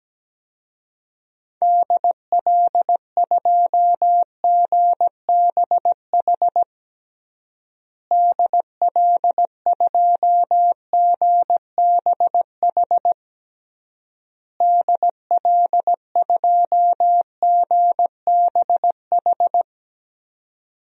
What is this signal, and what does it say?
Morse code, heard as a keyed tone, about 17 wpm, 700 Hz. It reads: DL2GBH DL2GBH DL2GBH
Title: DXCC-100 Call Signs - 6 Characters - Repeated 3x - 17wpm